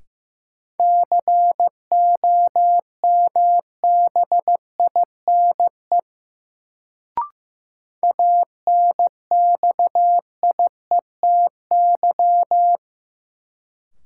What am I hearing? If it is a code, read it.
Morse code, heard as a keyed tone, about 15 wpm, 700 Hz. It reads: COMBINE ANXIETY
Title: Sentences from Top 1600 Words - New Word Drill - 15wpm